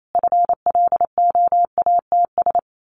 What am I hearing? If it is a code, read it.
Morse code, heard as a keyed tone, about 28 wpm, 700 Hz. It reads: FLOATS
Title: A Sound Of Thunder 28wpm